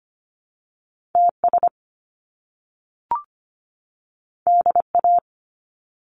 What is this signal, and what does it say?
Morse code, heard as a keyed tone, about 25 wpm, 700 Hz. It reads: TS DA
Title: Top Two Letter Combinations in Words 25wpm